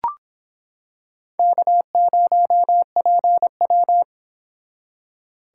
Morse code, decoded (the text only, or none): K0PW